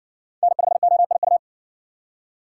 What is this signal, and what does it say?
Morse code, heard as a keyed tone, about 60 wpm, 700 Hz. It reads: N5OIU